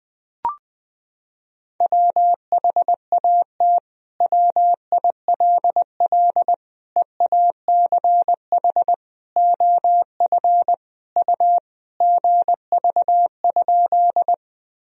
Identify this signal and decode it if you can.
Morse code, heard as a keyed tone, about 20 wpm, 700 Hz. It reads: WHAT WILL EACH OF U GV?
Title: Sentences from Top 200 Words and Abbreviations 20wpm